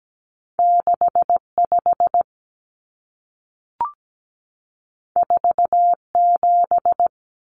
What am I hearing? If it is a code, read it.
Morse code, heard as a keyed tone, about 17 wpm, 700 Hz. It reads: 65 47
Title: Numbers - 2-Digit 17wpm